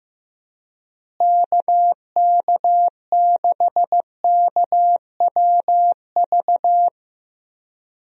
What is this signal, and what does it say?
Morse code, heard as a keyed tone, about 15 wpm, 700 Hz. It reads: KK6KWV
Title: USA Call Signs - 6 Characters - Repeated 3x - 15wpm